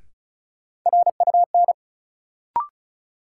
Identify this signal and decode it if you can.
Morse code, heard as a keyed tone, about 36 wpm, 700 Hz. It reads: RUN
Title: Sentences from Top 300 Words - New Word Drill - 35wpm